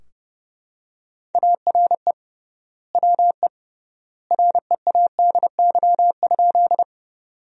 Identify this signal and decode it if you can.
Morse code, heard as a keyed tone, about 30 wpm, 700 Hz. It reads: ARE WE READY?
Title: 3x Word-Spacing - Sentences from Top 400 Words - 30wpm